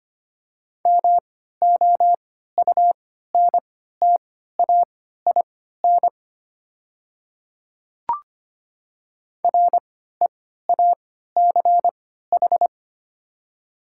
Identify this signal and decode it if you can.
Morse code, heard as a keyed tone, about 25 wpm, spaced as for 14 wpm, 700 Hz. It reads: MOUNTAIN REACH
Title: 3x Character Spacing - Top 500 Words - 25wpm